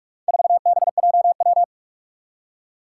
Morse code, heard as a keyed tone, about 41 wpm, 700 Hz. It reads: 4BJW
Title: Four Challenging Letter-Number Combinations 45wpm